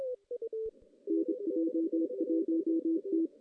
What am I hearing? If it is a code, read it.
Morse code, heard as a keyed tone, about 26 wpm, 330 Hz. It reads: NJ1T
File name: 2016 JAN CW NAQP / 2016 Jan naqp k5ot - 0-5